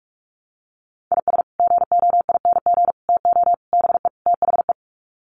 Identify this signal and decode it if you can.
Morse code, heard as a keyed tone, about 44 wpm, 700 Hz. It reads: IS GOING TO BE THE
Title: Sets of 5 Words 45wpm